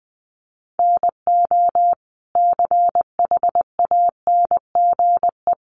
Morse code, heard as a keyed tone, about 20 wpm, 700 Hz. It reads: NO CHANGE
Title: Sentences from Top 1200 Words 20wpm